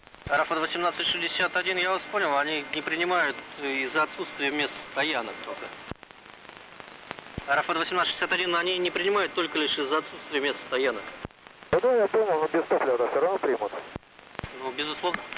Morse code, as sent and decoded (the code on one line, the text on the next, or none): none